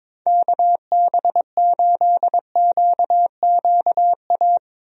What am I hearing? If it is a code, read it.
Morse code, heard as a keyed tone, about 22 wpm, 700 Hz. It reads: KB8QQA
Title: USA Call Signs - 6 Characters - Repeated 2x - 22wpm